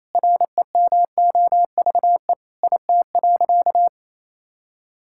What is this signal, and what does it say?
Morse code, heard as a keyed tone, about 27 wpm, 700 Hz. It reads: REMOVE IT.
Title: Sentences from Top 1000 Words - Review Part 5 - 28wpm